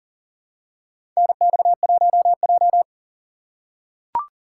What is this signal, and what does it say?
Morse code, heard as a keyed tone, about 40 wpm, 700 Hz. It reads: NX1J
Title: USA Call Signs - 4 Characters - 2x Character Spacing - 40wpm